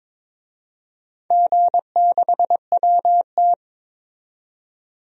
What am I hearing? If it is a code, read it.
Morse code, heard as a keyed tone, about 22 wpm, 700 Hz. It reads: G6WT
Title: DXCC-100 Call Signs - 4 Characters - 22wpm